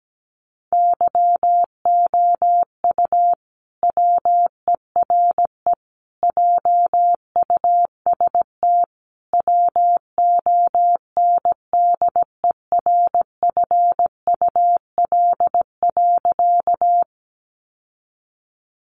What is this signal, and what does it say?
Morse code, heard as a keyed tone, about 17 wpm, 700 Hz. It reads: YOU WERE JUST WONDERFUL.
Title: Sentences from Top 1100 Words 17wpm